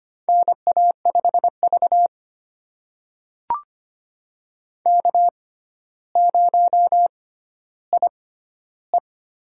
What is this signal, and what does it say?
Morse code, heard as a keyed tone, about 25 wpm, spaced as for 13 wpm, 700 Hz. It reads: NA5V K0IE
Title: USA Call Signs - 6x Character Spacing - 25wpm